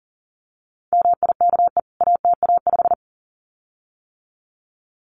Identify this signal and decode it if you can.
Morse code, heard as a keyed tone, about 40 wpm, 700 Hz. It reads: MIKE ATA5
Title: CWT Exchange 40wpm